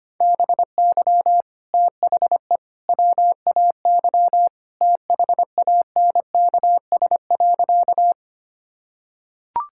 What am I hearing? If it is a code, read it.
Morse code, heard as a keyed tone, about 25 wpm, 700 Hz. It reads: BY THE WAY THANKS.
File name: practice / 25wpm / Sentences with 3-Word Phrases - 25wpm